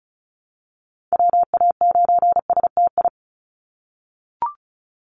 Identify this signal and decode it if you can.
Morse code, heard as a keyed tone, about 34 wpm, 700 Hz. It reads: WA9STI
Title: NA SOTA Activator Callsigns 35wpm